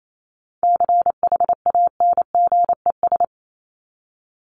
Morse code, heard as a keyed tone, about 28 wpm, 700 Hz. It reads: CHANGES
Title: ES:S Suffix - Top 100 Words with Base Words 28wpm